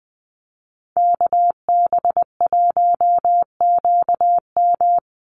Morse code, heard as a keyed tone, about 20 wpm, 700 Hz. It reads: KB1QM
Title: USA Call Signs 20wpm